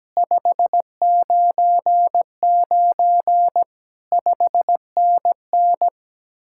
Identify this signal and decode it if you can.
Morse code, heard as a keyed tone, about 18 wpm, 700 Hz. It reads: 599 5NN
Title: Speed-Racing Common QSO Phrases Encore - 17wpm